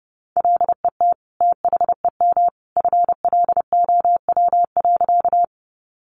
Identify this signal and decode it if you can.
Morse code, heard as a keyed tone, about 30 wpm, 700 Hz. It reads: LET THEM FLOW.
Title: Sentences from Top 1200 Words 30wpm